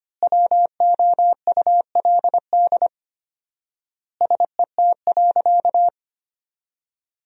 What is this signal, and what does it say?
Morse code, heard as a keyed tone, about 25 wpm, 700 Hz. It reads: WOULD SET.
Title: 4x Word-Spacing - Sentences from Top 400 Words - 25wpm